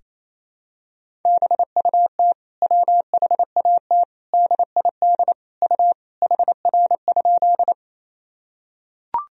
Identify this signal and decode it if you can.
Morse code, heard as a keyed tone, about 28 wpm, 700 Hz. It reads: BUT WHAT DID U HR?
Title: Sentences from Top 300 Words and Abbreviations 28wpm